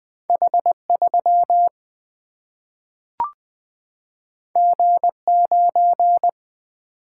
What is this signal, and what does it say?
Morse code, heard as a keyed tone, about 20 wpm, 700 Hz. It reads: H3 G9
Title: Single-Letter Single-Number 20wpm